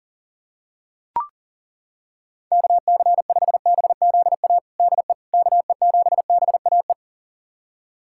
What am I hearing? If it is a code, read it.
Morse code, heard as a keyed tone, about 40 wpm, 700 Hz. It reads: KC5BZA DE KE7BAE
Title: USA Callsign DE Callsign 40wpm